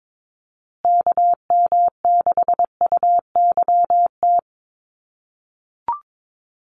Morse code, heard as a keyed tone, about 22 wpm, 700 Hz. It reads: KM6UYT
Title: USA Call Signs Repeated 2x 22wpm